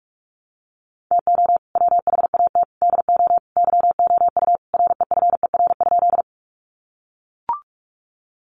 Morse code, heard as a keyed tone, about 44 wpm, 700 Hz. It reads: TO WHAT DO YOU REFER?
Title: Sentences from Top 1000 Words 45wpm